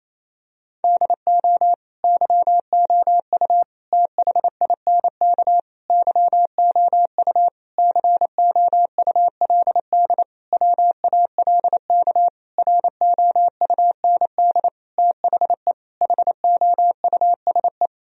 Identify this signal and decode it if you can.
Morse code, heard as a keyed tone, about 28 wpm, 700 Hz. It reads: DO YOU THINK YOU COULD WALK ROUND THE HOUSE
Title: Sentences from Top 1000 Words - Review Part 1 - 28wpm